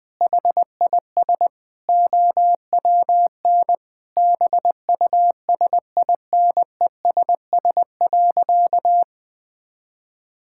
Morse code, heard as a keyed tone, about 21 wpm, 700 Hz. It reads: HIS OWN BUSINESS.
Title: Sentences from Top 1000 Words - Review Part 6 - 20wpm